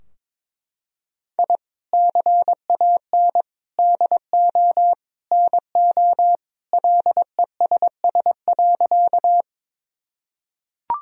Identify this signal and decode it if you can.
Morse code, heard as a keyed tone, about 22 wpm, 700 Hz. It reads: I CAN DO NO LESS.
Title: Sentences from Top 1000 Words - Review Part 2 - 22wpm